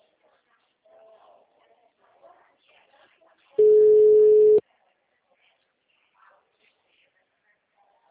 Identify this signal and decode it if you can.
Ringback tone (Europe, 425 Hz).